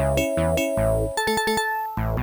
There is no code.